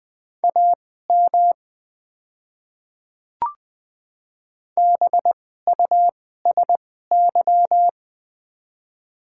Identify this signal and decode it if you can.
Morse code, heard as a keyed tone, about 20 wpm, spaced as for 14 wpm, 700 Hz. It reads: AM BUSY
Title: 2x Character Spacing - Top 500 Words - 20wpm